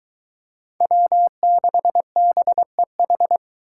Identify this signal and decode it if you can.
Morse code, heard as a keyed tone, about 23 wpm, 700 Hz. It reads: W6BEH